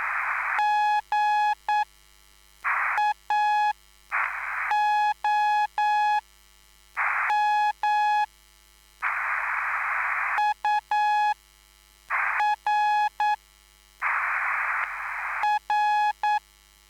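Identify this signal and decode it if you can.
Morse code, heard as a keyed tone, about 9 wpm, spaced as for 5 wpm, 840 Hz. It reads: GAOM UR R
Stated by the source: Live CW QSO between PA2BX and SP3JFK on the 14055 kHz QRS frequency, at a basic 8 wpm.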